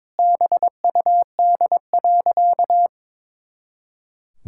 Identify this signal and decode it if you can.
Morse code, heard as a keyed tone, about 22 wpm, 700 Hz. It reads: BUD.
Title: Pangram Challenge v2 - 22wpm